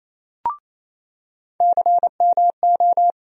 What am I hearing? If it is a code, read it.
Morse code, heard as a keyed tone, about 27 wpm, 700 Hz. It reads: CMO
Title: USA Call Sign Postfixes - 3 Letters 28wpm